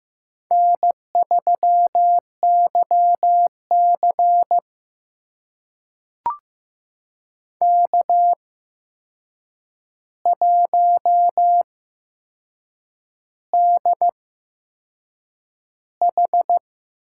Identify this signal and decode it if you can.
Morse code, heard as a keyed tone, about 15 wpm, spaced as for 6 wpm, 700 Hz. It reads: N3YC K1DH
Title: USA Call Signs - 4 Characters - 8x Character Spacing - 15wpm